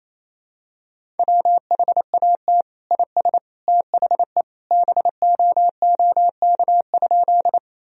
Morse code, heard as a keyed tone, about 28 wpm, 700 Hz. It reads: WHAT IS THE BOOK?
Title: Sentences from Top 1000 Words - Review Part 1 - 28wpm